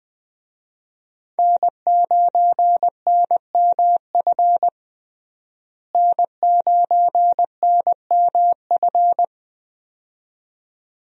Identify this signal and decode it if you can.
Morse code, heard as a keyed tone, about 20 wpm, 700 Hz. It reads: N9NMF N9NMF
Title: USA Call Signs - 5 Characters - Repeated 2x - 20wpm